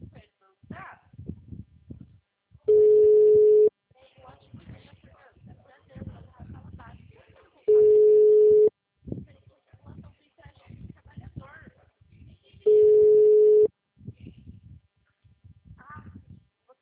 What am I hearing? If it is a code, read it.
Ringback tone (Europe, 425 Hz), with speech.